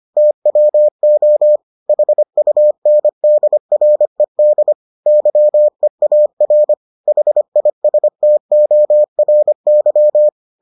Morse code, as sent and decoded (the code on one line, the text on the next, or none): - .-- --- / .... ..- -. -.. .-. -.. / -.-- . .- .-. / .... .. ... - --- .-. -.--
TWO HUNDRED YEAR HISTORY